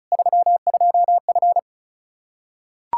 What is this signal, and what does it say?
Morse code, heard as a keyed tone, about 36 wpm, 700 Hz. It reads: 32F
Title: Practice QSO - Exchange 2 - Info - 35wpm